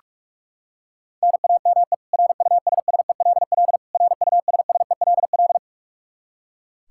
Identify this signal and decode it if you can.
Morse code, heard as a keyed tone, about 41 wpm, 700 Hz. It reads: NAME RUSSELL RUSSELL
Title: K1USN SST QSO Practice 45wpm